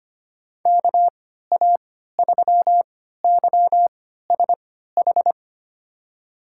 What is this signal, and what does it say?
Morse code, heard as a keyed tone, about 25 wpm, spaced as for 14 wpm, 700 Hz. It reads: KA3YSH
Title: USA Call Signs - 6 Characters - 3x Character Spacing - 25wpm